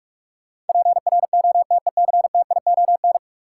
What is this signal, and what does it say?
Morse code, heard as a keyed tone, about 46 wpm, 700 Hz. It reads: PROTECTION
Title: Sentences from Top 1600 Words - New Word Drill Alt - 45wpm